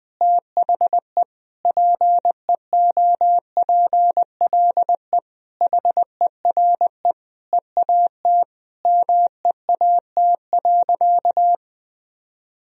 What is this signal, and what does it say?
Morse code, heard as a keyed tone, about 20 wpm, 700 Hz. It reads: THE PEOPLE HERE EAT MEAT.
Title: Sentences from Top 1000 Words - Review Part 3 - 20wpm